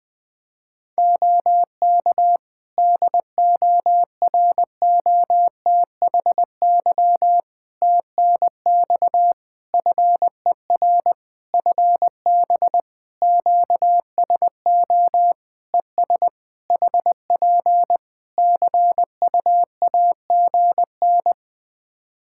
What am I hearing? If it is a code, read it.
Morse code, heard as a keyed tone, about 20 wpm, 700 Hz. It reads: OK DOROTHY TNX FER FB QSO ES HP CUAGN
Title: Practice QSO - Exchange 4 - Ending - 20wpm